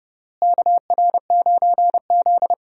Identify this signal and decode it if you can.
Morse code, heard as a keyed tone, about 30 wpm, 700 Hz. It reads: KR9Z
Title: USA Call Signs - 4 Characters - Repeated 3x - 30wpm